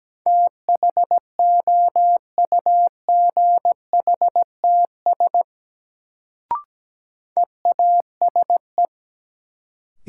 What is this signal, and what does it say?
Morse code, heard as a keyed tone, about 17 wpm, 700 Hz. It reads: THOUGHTS EASE
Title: Sentences from Top 1000 Words - Review Part 1 - New Word Drill - 17wpm